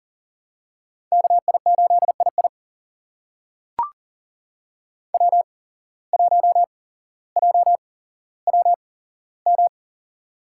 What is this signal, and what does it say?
Morse code, heard as a keyed tone, about 42 wpm, spaced as for 17 wpm, 700 Hz. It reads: KI8II W1JWM
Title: USA Call Signs - 8x Character Spacing - 40wpm